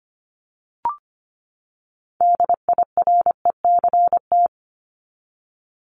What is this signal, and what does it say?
Morse code, heard as a keyed tone, about 25 wpm, 700 Hz. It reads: DIRECT
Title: Top 500 Words 25wpm